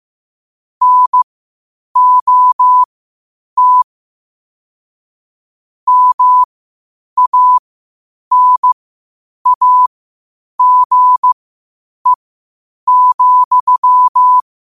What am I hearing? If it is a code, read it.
Morse code, heard as a keyed tone, about 15 wpm, spaced as for 8 wpm, 1000 Hz. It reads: NOT MANAGE,
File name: qotdfiles / 8wpm